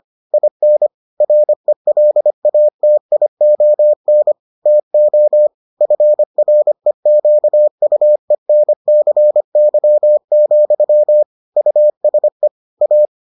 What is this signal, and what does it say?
Morse code, heard as a keyed tone, about 25 wpm, 600 Hz. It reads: IN RELATION TO FREQUENCY, USE A